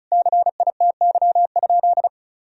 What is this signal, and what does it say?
Morse code, heard as a keyed tone, about 34 wpm, 700 Hz. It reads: CITY?
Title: Sentences from Top 1000 Words - Review Part 2 - 35wpm